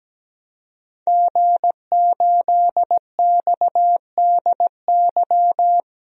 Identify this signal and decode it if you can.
Morse code, heard as a keyed tone, about 17 wpm, 700 Hz. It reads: G8XDY
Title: DXCC-100 Call Signs - 5 Characters - Repeated 3x - 17wpm